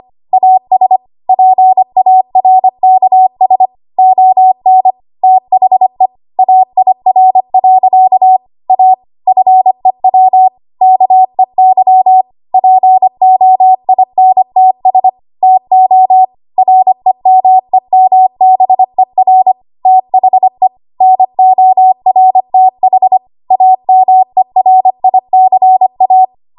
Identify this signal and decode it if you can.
Morse code, heard as a keyed tone, about 25 wpm, 750 Hz. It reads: AS PARKS ON THE AIR. A FEW KEY POINTS TO REMEMBER THE NORTH AMERICA
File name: Morse / 25WPM